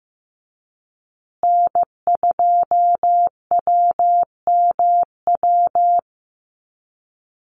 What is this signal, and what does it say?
Morse code, heard as a keyed tone, about 15 wpm, 700 Hz. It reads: N2WMW
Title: USA Call Signs Repeated 2x 15wpm